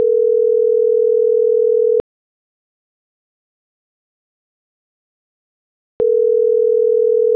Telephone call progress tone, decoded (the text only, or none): ringback tone